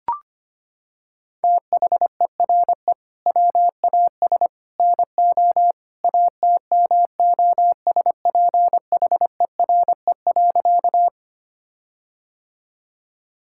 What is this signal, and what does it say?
Morse code, heard as a keyed tone, about 25 wpm, 700 Hz. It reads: THERE WAS NO ATMOSPHERE.
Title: Sentences from Top 1600 Words 25wpm